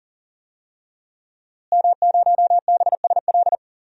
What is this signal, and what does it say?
Morse code, heard as a keyed tone, about 40 wpm, 700 Hz. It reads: M0BSL
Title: DXCC-100 Call Signs - 5 Characters - Repeated 2x - 40wpm